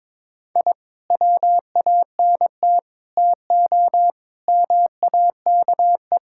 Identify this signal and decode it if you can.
Morse code, heard as a keyed tone, about 22 wpm, 700 Hz. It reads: I WANT TO MAKE
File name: practice / 22wpm / Sets of 7 Words - 22wpm